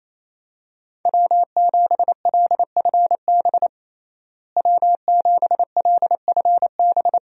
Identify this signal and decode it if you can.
Morse code, heard as a keyed tone, about 28 wpm, 700 Hz. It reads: W7LFB W7LFB
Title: USA Call Signs - 5 Characters - Repeated 3x - 28wpm